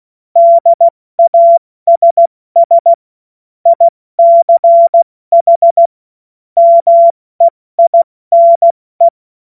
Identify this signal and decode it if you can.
Morse code, heard as a keyed tone, about 16 wpm, 680 Hz. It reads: DASS ICH MEINE
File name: Morsekurs / Witze / Witz 22. Juni 2021, 70 F80